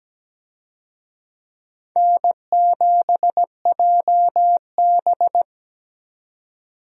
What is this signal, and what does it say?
Morse code, heard as a keyed tone, about 17 wpm, 700 Hz. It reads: N7JB
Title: USA Call Signs - 4 Characters - 17wpm